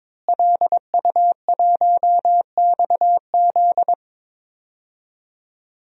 Morse code, heard as a keyed tone, about 22 wpm, 700 Hz. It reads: LU1XZ
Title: DXCC-100 Call Signs - 5 Characters - Repeated 2x - 22wpm